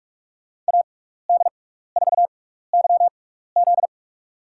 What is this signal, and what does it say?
Morse code, heard as a keyed tone, about 46 wpm, spaced as for 16 wpm, 700 Hz. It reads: AD4YZ
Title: USA Call Signs - 6x Character Spacing - 45wpm